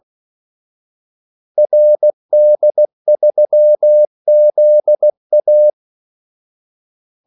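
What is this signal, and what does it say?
Morse code, heard as a keyed tone, about 16 wpm, 600 Hz. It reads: RD3ZA
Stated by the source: Morse call sign practice at 16 wpm.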